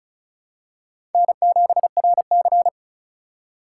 Morse code, heard as a keyed tone, about 36 wpm, 700 Hz. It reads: N7RC
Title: USA Call Signs - 4 Characters - 2x Character Spacing - 35wpm